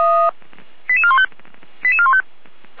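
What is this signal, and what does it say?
Five-tone selective call (ZVEI), heard as quick runs of five tones, tones of about 70 ms. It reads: 88415 88415
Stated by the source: Five-tone ZVEI selective call received on an SDR.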